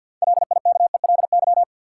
Morse code, heard as a keyed tone, about 53 wpm, 700 Hz. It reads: LIKELY
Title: Sets of 2 Words - New Word Drill - 50wpm